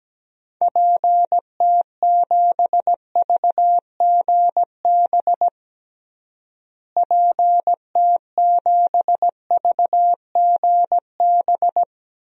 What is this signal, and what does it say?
Morse code, heard as a keyed tone, about 17 wpm, 700 Hz. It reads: PT7VGB PT7VGB
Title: DXCC-100 Call Signs - 6 Characters - Repeated 2x - 17wpm